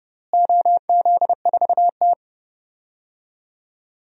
Morse code, heard as a keyed tone, about 30 wpm, 700 Hz. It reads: OZ4T
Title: DXCC-100 Call Signs - 4 Characters - Repeated 2x - 30wpm